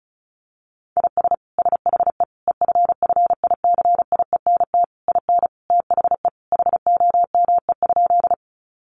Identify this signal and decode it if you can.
Morse code, heard as a keyed tone, about 34 wpm, 700 Hz. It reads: IS SHE EFFICIENT IN THE HOME?
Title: Sentences from Top 1300 Words 35wpm